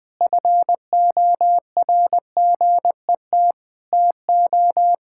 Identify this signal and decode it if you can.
Morse code, heard as a keyed tone, about 20 wpm, 700 Hz. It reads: FORGET TO
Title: Sentences with 5-Word Phrases - 20wpm